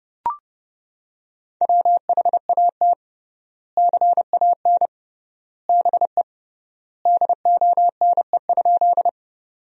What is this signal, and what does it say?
Morse code, heard as a keyed tone, about 30 wpm, 700 Hz. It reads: WHAT CAN BE DONE?
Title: 3x Word-Spacing - Sentences from Top 500 Words - 30wpm